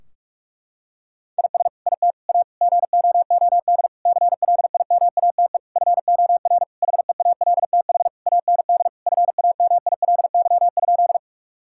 Morse code, heard as a keyed tone, about 46 wpm, 700 Hz. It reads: IS IT A GOOD CLIMATE FOR HEALTH AND FAMILY?
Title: Sentences from Top 1600 Words 45wpm